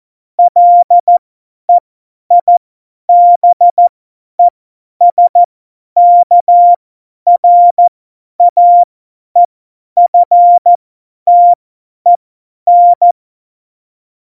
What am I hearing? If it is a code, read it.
Morse code, heard as a keyed tone, about 14 wpm, spaced as for 10 wpm, 700 Hz. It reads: LEIBESKRAEFTEN
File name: Morsekurs / Witze / Witz 18. September 2018, 50 F70